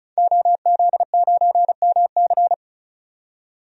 Morse code, heard as a keyed tone, about 35 wpm, 700 Hz. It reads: OZ9MC
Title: DXCC-100 Call Signs - 5 Characters - 35wpm